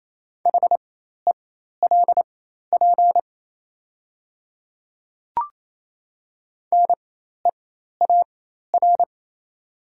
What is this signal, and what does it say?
Morse code, heard as a keyed tone, about 28 wpm, spaced as for 13 wpm, 700 Hz. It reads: HELP NEAR
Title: 4x Character Spacing - Top 500 Words - 28wpm